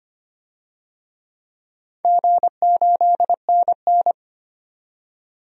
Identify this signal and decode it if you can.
Morse code, heard as a keyed tone, about 25 wpm, 700 Hz. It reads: G8NN
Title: DXCC-100 Call Signs - 4 Characters - 25wpm